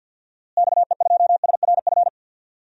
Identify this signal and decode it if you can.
Morse code, heard as a keyed tone, about 53 wpm, 700 Hz. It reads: XE2SRF
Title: DXCC-100 Call Signs - 6 Characters - Repeated 3x - 50wpm